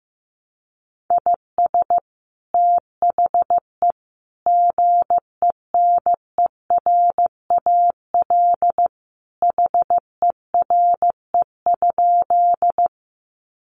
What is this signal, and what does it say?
Morse code, heard as a keyed tone, about 15 wpm, 700 Hz. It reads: IS THE GENERAL HERE?
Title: Sentences from Top 600 Words 15wpm